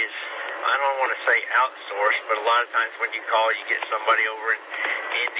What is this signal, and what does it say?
Emergency Alert System (SAME) tones, heard as harsh buzzing bursts, with speech. It is heard from 1.1 to 5.1 s.